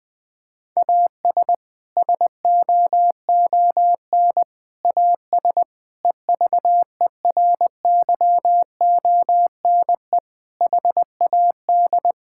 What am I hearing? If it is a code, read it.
Morse code, heard as a keyed tone, about 19 wpm, 700 Hz. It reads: AS SOON AS EVERYONE HAD
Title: Sentences from Top 1000 Words - Review Part 5 - 20wpm